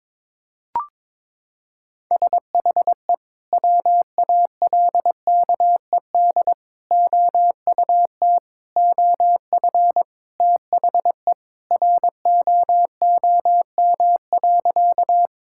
Morse code, heard as a keyed tone, about 22 wpm, 700 Hz. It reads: SHE WALKED OUT OF THE ROOM.
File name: practice / 22wpm / Sentences with 3-Word Phrases - 22wpm